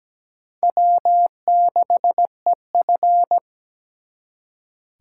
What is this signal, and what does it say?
Morse code, heard as a keyed tone, about 20 wpm, 700 Hz. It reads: W6EF
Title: Speed Racing - USA 1x2 Call Signs 17wpm